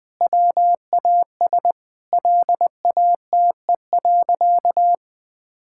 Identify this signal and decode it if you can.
Morse code, heard as a keyed tone, about 19 wpm, 700 Hz. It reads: WAS LATE.
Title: Sentences with 3-Word Phrases - Spoken Beginning - 20wpm